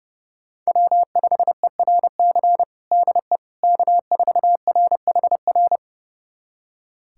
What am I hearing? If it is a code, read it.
Morse code, heard as a keyed tone, about 30 wpm, 700 Hz. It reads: W5ERC DE K4RHR
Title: USA Callsign DE Callsign 30wpm